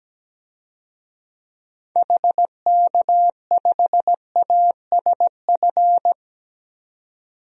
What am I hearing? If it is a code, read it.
Morse code, heard as a keyed tone, about 17 wpm, 700 Hz. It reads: HK5ASF
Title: DXCC-100 Call Signs - 6 Characters - 17wpm